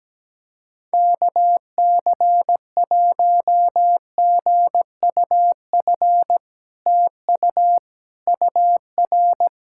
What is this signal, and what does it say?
Morse code, heard as a keyed tone, about 17 wpm, 700 Hz. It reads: KC1GUF TU UR
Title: SOTA QSO Practice 17wpm